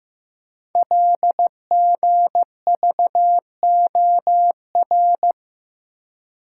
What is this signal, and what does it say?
Morse code, heard as a keyed tone, about 15 wpm, 700 Hz. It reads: LGVOR
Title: Code Groups - 5 Letters - 15wpm